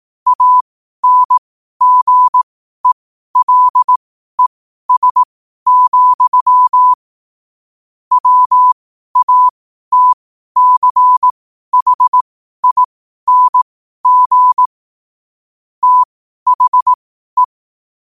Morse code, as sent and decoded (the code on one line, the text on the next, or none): .- -. --. .-.. ... --..-- / .-- .- - -.-. .... .. -. --. / - .... .
ANGELES, WATCHING THE